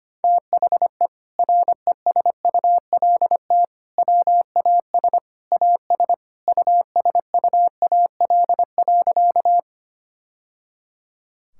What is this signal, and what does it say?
Morse code, heard as a keyed tone, about 25 wpm, 700 Hz. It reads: THE RESULT WAS AS USUAL.
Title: Sentences from Top 1000 Words - Review Part 5 - 25wpm